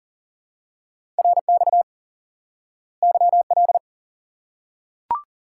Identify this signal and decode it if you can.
Morse code, heard as a keyed tone, about 40 wpm, 700 Hz. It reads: RX YL
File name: IQE Course 3 - Lesson 89 - YAGI - Two Elements - 40wpm